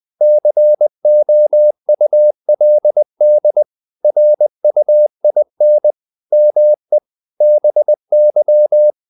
Morse code, heard as a keyed tone, about 20 wpm, 600 Hz. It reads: COULD RUIN ME BY
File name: the old man and the sea / the old man and the sea part 1 20wpm